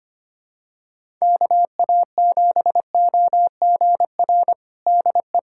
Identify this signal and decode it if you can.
Morse code, heard as a keyed tone, about 25 wpm, 700 Hz. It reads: KA7OGR DE
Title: USA Callsign DE Callsign - 6 Characters - 25wpm